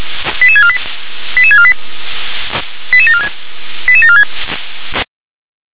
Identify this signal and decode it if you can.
Five-tone selective call (ZVEI), heard as quick runs of five tones, tones of about 70 ms. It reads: 88648 88648 88746 88746
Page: SDR/ZVEI